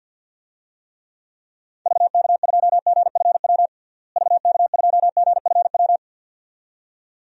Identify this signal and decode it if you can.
Morse code, heard as a keyed tone, about 53 wpm, 700 Hz. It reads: VK2GFW VK2GFW
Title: DXCC-100 Call Signs - 6 Characters - Repeated 2x - 50wpm